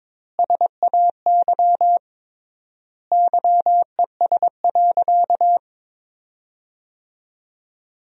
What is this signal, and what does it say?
Morse code, heard as a keyed tone, about 22 wpm, 700 Hz. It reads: SAY YES.
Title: 3x Word-Spacing - Sentences from Top 500 Words - 22wpm